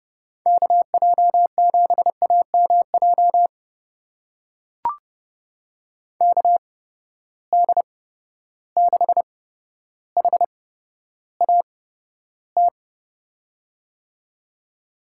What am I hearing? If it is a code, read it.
Morse code, heard as a keyed tone, about 30 wpm, spaced as for 13 wpm, 700 Hz. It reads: KJ7AMJ KD6HAT
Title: USA Call Signs - 6 Characters - 8x Character Spacing - 30wpm